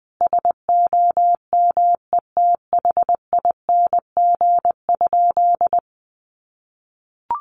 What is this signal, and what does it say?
Morse code, heard as a keyed tone, about 20 wpm, 700 Hz. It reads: SOMETHING?
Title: Sentences from Top 1100 Words 20wpm